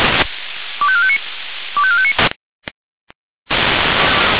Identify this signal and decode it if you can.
Five-tone selective call (ZVEI), heard as quick runs of five tones, tones of about 70 ms. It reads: 25459 25459